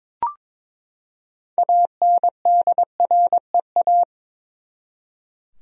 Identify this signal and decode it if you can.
Morse code, heard as a keyed tone, about 21 wpm, 700 Hz. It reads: ANDREA
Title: USA Common Names - 22wpm